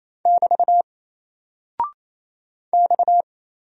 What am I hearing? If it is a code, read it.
Morse code, heard as a keyed tone, about 27 wpm, 700 Hz. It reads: <BT> X
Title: Single Character (Letter:Number:Punctuation:Prosign) 28wpm